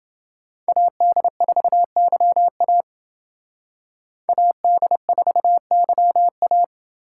Morse code, heard as a keyed tone, about 29 wpm, 700 Hz. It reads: AD4YA AD4YA